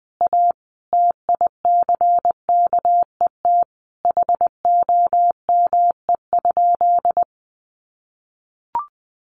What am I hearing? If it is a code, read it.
Morse code, heard as a keyed tone, about 20 wpm, 700 Hz. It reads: A TICKET HOME?
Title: Sentences from Top 1400 Words 20wpm